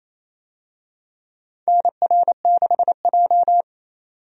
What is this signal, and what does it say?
Morse code, heard as a keyed tone, about 28 wpm, 700 Hz. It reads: NR6J
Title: USA Call Signs - 4 Characters - 28wpm